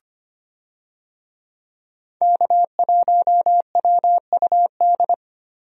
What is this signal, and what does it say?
Morse code, heard as a keyed tone, about 25 wpm, 700 Hz. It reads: K1WUD
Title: USA Call Signs - 5 Characters - 25wpm